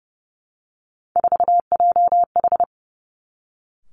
Morse code, heard as a keyed tone, about 30 wpm, 700 Hz. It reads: 4JH
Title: Three Challenging Letter-Number Combinations 30wpm